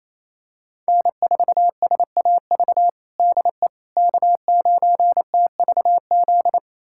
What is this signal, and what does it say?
Morse code, heard as a keyed tone, about 28 wpm, 700 Hz. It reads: N4SAV DE K9TVZ
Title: USA Callsign DE Callsign - 5 Characters - 28wpm